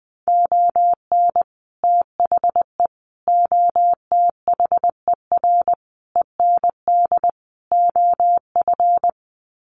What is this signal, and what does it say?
Morse code, heard as a keyed tone, about 20 wpm, 700 Hz. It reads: ON THE OTHER END OF
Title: Sets of 5 Words Encore 20wpm